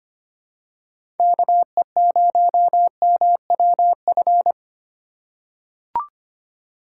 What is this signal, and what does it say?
Morse code, heard as a keyed tone, about 25 wpm, 700 Hz. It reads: KE0MWF
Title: USA Call Signs - 2x Character Spacing - 25wpm